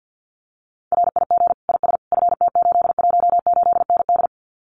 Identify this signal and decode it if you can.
Morse code, heard as a keyed tone, about 50 wpm, 700 Hz. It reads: RIG IS FT818ND
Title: Common QSO Phrases Encore 50wpm